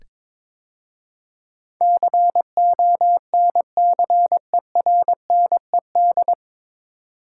Morse code, heard as a keyed tone, about 23 wpm, 700 Hz. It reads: CONCERNED
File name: 9 Letter Words - First 3 Letters Given - 22wpm